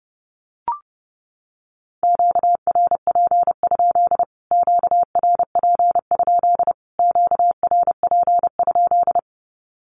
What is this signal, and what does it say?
Morse code, heard as a keyed tone, about 29 wpm, 700 Hz. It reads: QRP? QRP? QRP?
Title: Q-Codes Repeated 3x 30wpm